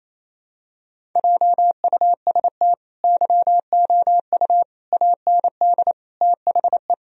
Morse code, heard as a keyed tone, about 28 wpm, 700 Hz. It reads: JUST YOU AND THE
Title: Sentences from Top 1000 Words - Review Part 1 - 28wpm